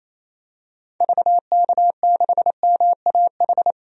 Morse code, heard as a keyed tone, about 28 wpm, 700 Hz. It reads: VK6MAH
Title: DXCC-100 Call Signs - 6 Characters - 28wpm